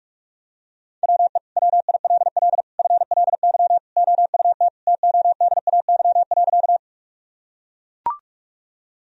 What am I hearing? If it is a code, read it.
Morse code, heard as a keyed tone, about 41 wpm, 700 Hz. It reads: WE WILL FLY OUT TODAY.
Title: Sentences from Top 1000 Words - Review Part 2 - 45wpm